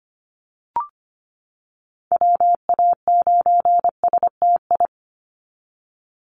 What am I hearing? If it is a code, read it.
Morse code, heard as a keyed tone, about 25 wpm, 700 Hz. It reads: WA9STI
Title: NA SOTA Activator Callsigns 25wpm